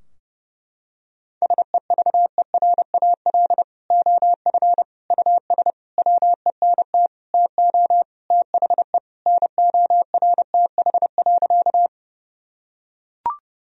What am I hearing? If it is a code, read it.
Morse code, heard as a keyed tone, about 30 wpm, 700 Hz. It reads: SEVERAL OF US WENT TO THE NORTH.
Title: Sentences from Top 1000 Words - Review Part 2 - 30wpm